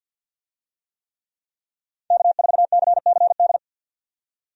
Morse code, heard as a keyed tone, about 51 wpm, 700 Hz. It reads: K4CCD
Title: USA Call Signs - 5 Characters - 50wpm